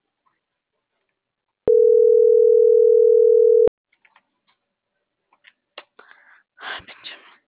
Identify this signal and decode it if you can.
Ringback tone (North America), 440 + 480 Hz.